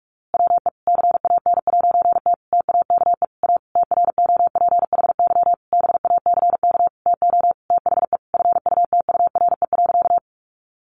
Morse code, heard as a keyed tone, about 44 wpm, 700 Hz. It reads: WE CAN'T TAKE A TROPHY BACK TO THE FUTURE.
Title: A Sound Of Thunder 45wpm